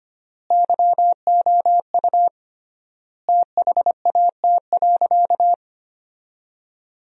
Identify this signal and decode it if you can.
Morse code, heard as a keyed tone, about 25 wpm, 700 Hz. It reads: YOU THAT.
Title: 3x Word-Spacing - Sentences from Top 300 Words - 25wpm